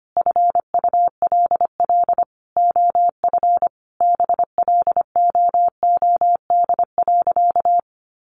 Morse code, heard as a keyed tone, about 25 wpm, 700 Hz. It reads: FULL OF BLOOD.
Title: Sentences from Top 1100 Words 25wpm